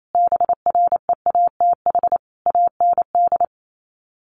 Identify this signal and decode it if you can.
Morse code, heard as a keyed tone, about 28 wpm, 700 Hz. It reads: BREATH AND
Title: Sets of 5 Words Encore 28wpm